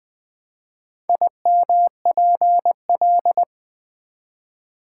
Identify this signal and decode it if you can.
Morse code, heard as a keyed tone, about 20 wpm, 700 Hz. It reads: IMPL
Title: Build A Word - 1 Letter At A Time - 5-Letter Words - 20wpm